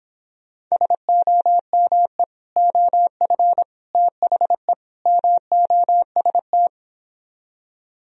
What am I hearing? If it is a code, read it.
Morse code, heard as a keyed tone, about 26 wpm, 700 Hz. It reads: SOME OF THE MOST